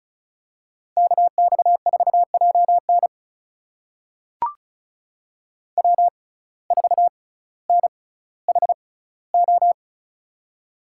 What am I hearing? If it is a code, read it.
Morse code, heard as a keyed tone, about 36 wpm, spaced as for 18 wpm, 700 Hz. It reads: KX4JN W4NHO
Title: USA Call Signs - 5 Characters - 6x Character Spacing - 35wpm